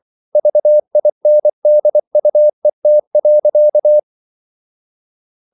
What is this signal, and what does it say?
Morse code, heard as a keyed tone, about 24 wpm, 600 Hz. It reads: VINDUET.
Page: Ferie 24 wpm